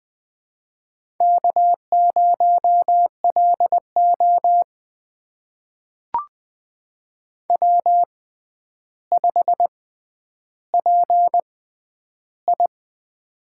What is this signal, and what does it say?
Morse code, heard as a keyed tone, about 20 wpm, spaced as for 10 wpm, 700 Hz. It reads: K0LO W5PI
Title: USA Call Signs - 4 Characters - 6x Character Spacing - 20wpm